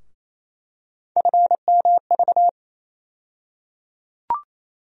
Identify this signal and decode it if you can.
Morse code, heard as a keyed tone, about 28 wpm, 700 Hz. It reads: FMV
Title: USA Call Sign Postfixes - 3 Letters 28wpm